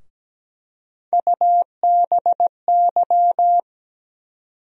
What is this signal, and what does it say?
Morse code, heard as a keyed tone, about 17 wpm, 700 Hz. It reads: UBY
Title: USA Call Sign Postfixes - 3 Letters 17wpm